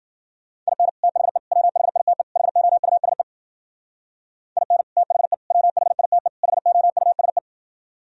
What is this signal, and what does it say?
Morse code, heard as a keyed tone, about 52 wpm, 700 Hz. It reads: IN THE WHITE HOUSE IN THE WHITE HOUSE